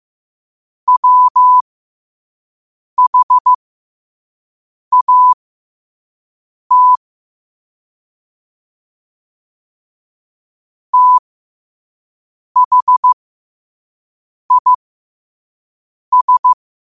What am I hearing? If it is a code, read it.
Morse code, heard as a keyed tone, about 15 wpm, spaced as for 5 wpm, 1000 Hz. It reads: WHAT THIS
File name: qotdfiles / 5wpm